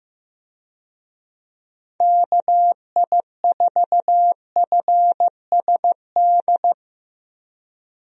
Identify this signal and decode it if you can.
Morse code, heard as a keyed tone, about 15 wpm, 700 Hz. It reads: KI4FSD
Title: USA Call Signs - 6 Characters - 15wpm